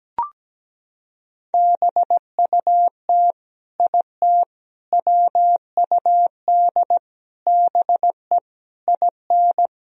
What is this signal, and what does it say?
Morse code, heard as a keyed tone, about 17 wpm, 700 Hz. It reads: BUT IT WUD BE IN
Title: Sentences from Top 300 Words and Abbreviations 17wpm